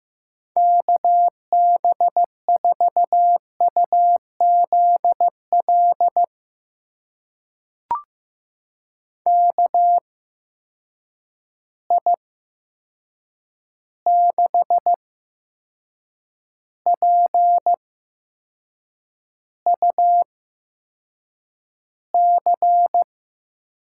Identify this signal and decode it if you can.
Morse code, heard as a keyed tone, about 15 wpm, spaced as for 6 wpm, 700 Hz. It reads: KB4UZL KI6PUC
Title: USA Call Signs - 6 Characters - 8x Character Spacing - 15wpm